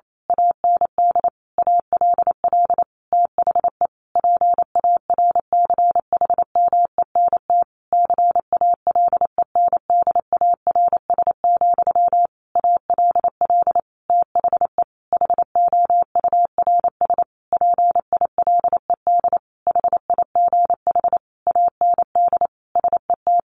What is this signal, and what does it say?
Morse code, heard as a keyed tone, about 28 wpm, 700 Hz. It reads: AND ALL THE PARCHMENT CALENDARS, ALL THE HOURS PILED HIGH AND SET
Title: A Sound Of Thunder 28wpm